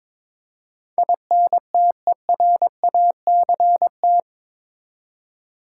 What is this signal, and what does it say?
Morse code, heard as a keyed tone, about 22 wpm, 700 Hz. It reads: INTERACT
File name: practice / 22wpm / Eight Letter Words - 22wpm